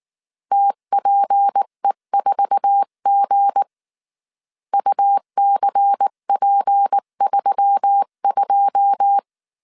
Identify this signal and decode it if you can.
Morse code, heard as a keyed tone, about 19 wpm, 800 Hz. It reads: TPE4G UCP32
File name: CW / DET / 1 9 1500C S B 18wpm 2551